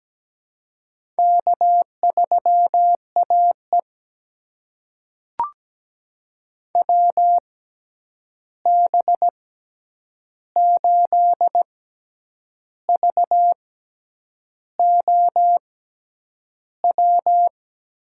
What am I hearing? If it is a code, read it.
Morse code, heard as a keyed tone, about 17 wpm, spaced as for 8 wpm, 700 Hz. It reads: K3AE WB8VOW
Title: USA Call Signs - 6x Character Spacing - 17wpm